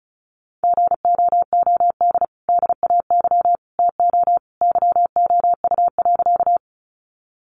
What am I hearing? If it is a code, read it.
Morse code, heard as a keyed tone, about 34 wpm, 700 Hz. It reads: GOOD DAY TO YOU.
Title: Sentences from Top 200 Words 35wpm